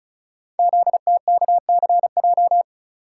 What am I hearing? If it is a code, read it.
Morse code, heard as a keyed tone, about 36 wpm, 700 Hz. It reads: ZTKCJ
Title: Code Groups - 5 Letters - 35wpm